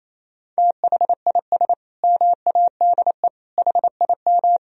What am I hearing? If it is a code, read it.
Morse code, heard as a keyed tone, about 28 wpm, 700 Hz. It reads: THIS MADE HIM
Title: Sentences from Top 500 Words and Abbreviations 28wpm